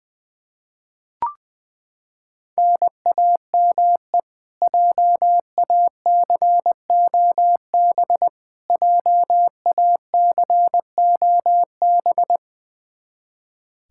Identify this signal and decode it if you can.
Morse code, heard as a keyed tone, about 20 wpm, 700 Hz. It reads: NAME JACOB JACOB
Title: K1USN SST QSO Practice 20wpm